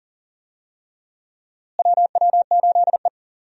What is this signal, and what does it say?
Morse code, heard as a keyed tone, about 39 wpm, 700 Hz. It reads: WW8E
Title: USA Call Signs - 4 Characters - Repeated 3x - 40wpm